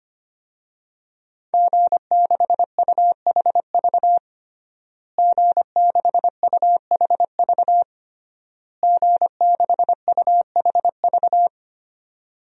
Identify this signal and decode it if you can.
Morse code, heard as a keyed tone, about 25 wpm, 700 Hz. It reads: G6UHV G6UHV G6UHV
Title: DXCC-100 Call Signs - 5 Characters - Repeated 3x - 25wpm